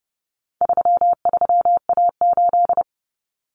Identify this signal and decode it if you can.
Morse code, heard as a keyed tone, about 30 wpm, 700 Hz. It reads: 33A8
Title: Common Cut Numbers - 4-Digit 30wpm